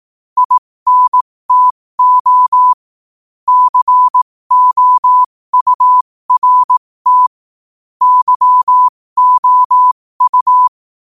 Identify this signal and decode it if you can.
Morse code, heard as a keyed tone, about 18 wpm, spaced as for 15 wpm, 1000 Hz. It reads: INTO COURT YOU
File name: qotdfiles / 15wpm